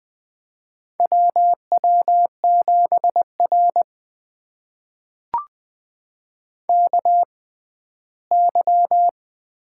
Morse code, heard as a keyed tone, about 20 wpm, spaced as for 14 wpm, 700 Hz. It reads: WW7R KY
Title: USA Call Signs - 4 Characters - 6x Character Spacing - 20wpm